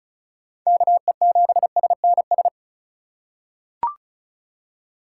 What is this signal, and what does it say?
Morse code, heard as a keyed tone, about 36 wpm, 700 Hz. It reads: KE7SNS
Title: USA Call Signs - 6 Characters - 2x Character Spacing - 35wpm